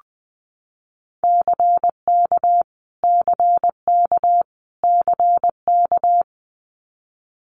Morse code, heard as a keyed tone, about 20 wpm, 700 Hz. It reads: CK CK CK
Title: Multi-letter Phonemes Repeated 3x 20wpm